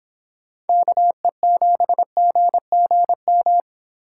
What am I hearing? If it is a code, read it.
Morse code, heard as a keyed tone, about 25 wpm, 700 Hz. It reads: KE7GGM